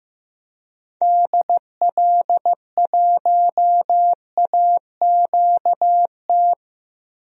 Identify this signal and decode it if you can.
Morse code, heard as a keyed tone, about 15 wpm, 700 Hz. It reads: DL1AQT
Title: DXCC-100 Call Signs - 6 Characters - 15wpm